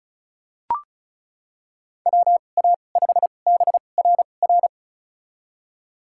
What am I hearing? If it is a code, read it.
Morse code, heard as a keyed tone, about 36 wpm, spaced as for 25 wpm, 700 Hz. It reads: WA5BRR
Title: USA Call Signs - 6 Characters - 2x Character Spacing - 35wpm